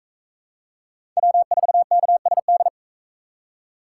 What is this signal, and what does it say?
Morse code, heard as a keyed tone, about 38 wpm, 700 Hz. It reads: W4KSD